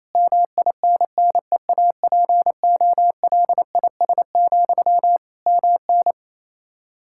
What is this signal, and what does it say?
Morse code, heard as a keyed tone, about 28 wpm, 700 Hz. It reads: MINNEAPOLIS, MN
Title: Practice QSO - Exchange 1 - Beginning - 28wpm